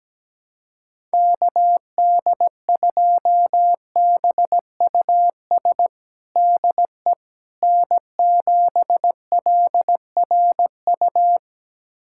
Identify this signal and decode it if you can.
Morse code, heard as a keyed tone, about 17 wpm, 700 Hz. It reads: KD2BUS DE N7LRU
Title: USA Callsign DE Callsign 17wpm